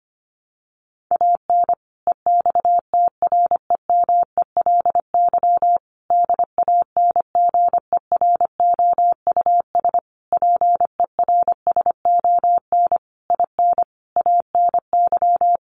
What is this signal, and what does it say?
Morse code, heard as a keyed tone, about 25 wpm, 700 Hz. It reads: AN EXTREMELY DANGEROUS PERSON IN ANY
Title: Sentences from Top 1200 Words 25wpm